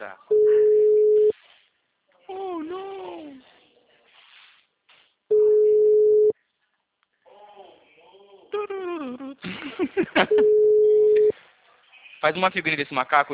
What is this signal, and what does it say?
Ringback tone (Europe, 425 Hz), with speech.